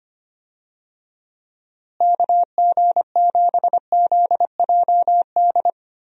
Morse code, heard as a keyed tone, about 25 wpm, 700 Hz. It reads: KG7ZJD
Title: USA Call Signs - 6 Characters - Repeated 3x - 25wpm